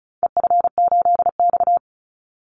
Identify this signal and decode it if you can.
Morse code, heard as a keyed tone, about 34 wpm, 700 Hz. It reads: EF8X
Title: Call Signs - 35wpm